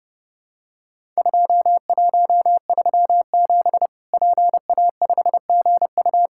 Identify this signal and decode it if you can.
Morse code, heard as a keyed tone, about 30 wpm, 700 Hz. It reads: 2137 PA5GU
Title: Worked All Europe DX Contest - QTC Exchange - 30wpm